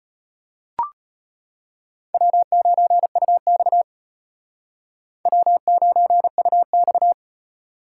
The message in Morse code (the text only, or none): W9UX W9UX